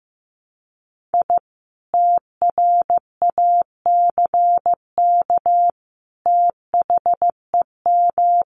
Morse code, heard as a keyed tone, about 15 wpm, 700 Hz. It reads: I TRACK THEM
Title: A Sound Of Thunder 15wpm